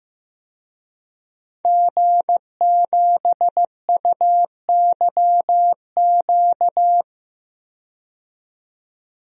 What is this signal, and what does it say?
Morse code, heard as a keyed tone, about 15 wpm, 700 Hz. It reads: G7UYQ
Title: DXCC-100 Call Signs - 5 Characters - 15wpm